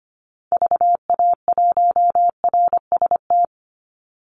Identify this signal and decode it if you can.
Morse code, heard as a keyed tone, about 25 wpm, 700 Hz. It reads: VA1RST
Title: Call Signs Repeated 2x - 25wpm